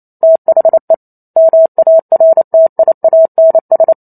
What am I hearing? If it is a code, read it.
Morse code, heard as a keyed tone, about 28 wpm, 650 Hz. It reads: THE MARTIANS